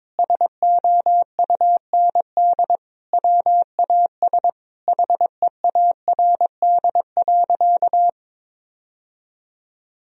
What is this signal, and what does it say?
Morse code, heard as a keyed tone, about 22 wpm, 700 Hz. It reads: SOUND WAS HEARD.
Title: Sentences from Top 1000 Words - Review Part 1 - 22wpm